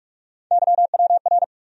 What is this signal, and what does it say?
Morse code, heard as a keyed tone, about 46 wpm, 700 Hz. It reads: YWR
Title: USA Call Sign Postfixes - 3 Letters 45wpm